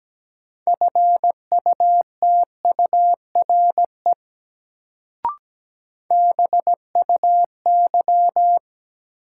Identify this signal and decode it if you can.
Morse code, heard as a keyed tone, about 17 wpm, 700 Hz. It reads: FUTURE BUY
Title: Sentences from Top 700 Words - New Word Drill - 17wpm